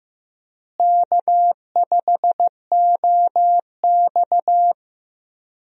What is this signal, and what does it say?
Morse code, heard as a keyed tone, about 15 wpm, 700 Hz. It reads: K5OX
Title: USA Call Signs - 4 Characters - Repeated 3x - 15wpm